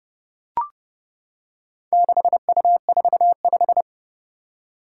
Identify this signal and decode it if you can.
Morse code, heard as a keyed tone, about 30 wpm, 700 Hz. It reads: 6U45
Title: Four Challenging Letter-Number Combinations 30wpm